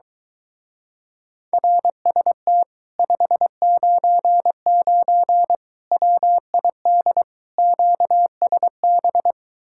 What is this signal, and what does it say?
Morse code, heard as a keyed tone, about 23 wpm, 700 Hz. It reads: RST 599 WID QSB